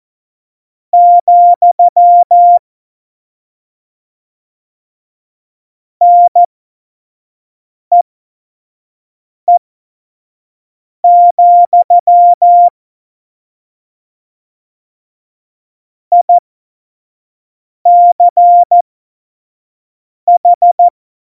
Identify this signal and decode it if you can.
Morse code, heard as a keyed tone, about 14 wpm, spaced as for 5 wpm, 700 Hz. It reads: , NEE, ICH
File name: Morsekurs / Witze / Witz 10.Juli 2018, 25 F70